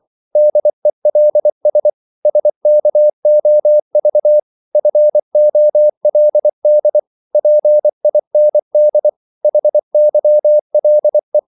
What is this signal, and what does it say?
Morse code, heard as a keyed tone, about 24 wpm, 600 Hz. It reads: DELS SKOV FOLD PIND HYLE
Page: Korte danske ord 24wpm